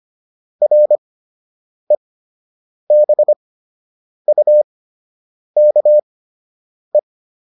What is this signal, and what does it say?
Morse code, heard as a keyed tone, about 25 wpm, spaced as for 8 wpm, 600 Hz. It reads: REBUKE